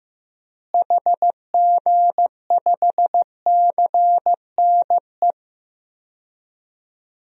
Morse code, heard as a keyed tone, about 16 wpm, 700 Hz. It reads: HG5CNE
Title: DXCC-100 Call Signs - 6 Characters - Repeated 2x - 15wpm